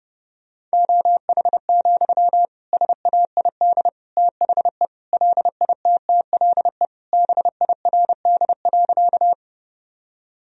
Morse code, heard as a keyed tone, about 30 wpm, 700 Hz. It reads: OH, SAID THE LITTLE BIRD.
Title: Sentences from Top 1000 Words - Review Part 2 - 30wpm